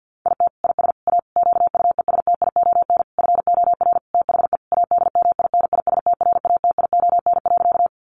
Morse code, heard as a keyed tone, about 50 wpm, 700 Hz. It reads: IT IS A QUESTION FOR THE ADMINISTRATION.
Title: Sentences from Top 1300 Words 50wpm